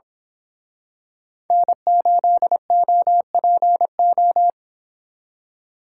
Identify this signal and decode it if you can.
Morse code, heard as a keyed tone, about 25 wpm, 700 Hz. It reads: N8OPO
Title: Speed Racing - USA 1x3 Call Signs 22wpm